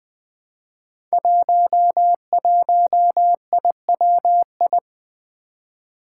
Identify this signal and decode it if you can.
Morse code, heard as a keyed tone, about 20 wpm, 700 Hz. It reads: 11IWI
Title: Five Challenging Letter-Number Combinations 20wpm